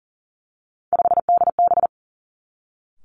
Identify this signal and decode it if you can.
Morse code, heard as a keyed tone, about 38 wpm, 690 Hz. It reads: 5DB
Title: Three Challenging Letter-Number Combinations 40wpm